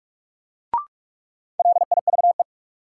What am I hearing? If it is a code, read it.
Morse code, heard as a keyed tone, about 42 wpm, 700 Hz. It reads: LIVE
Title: Sentences from Top 200 Words - New Word Drill - 45wpm